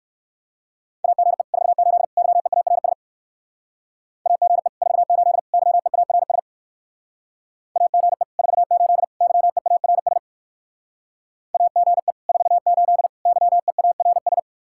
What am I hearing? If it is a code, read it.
Morse code, heard as a keyed tone, about 56 wpm, 700 Hz. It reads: AGE 48 YEARS AGE 48 YEARS AGE 48 YEARS AGE 48 YEARS